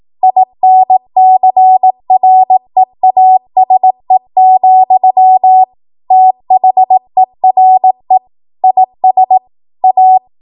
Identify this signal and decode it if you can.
Morse code, heard as a keyed tone, about 18 wpm, 750 Hz. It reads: INCREASE, THERE IS A
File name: Archive / 18 WPM / 18WPM